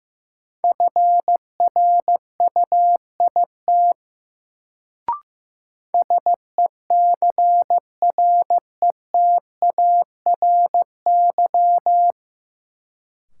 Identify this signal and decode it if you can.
Morse code, heard as a keyed tone, about 16 wpm, 700 Hz. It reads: FRUIT SECRETARY
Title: Sentences from Top 1300 Words - New Word Drill Alt - 15wpm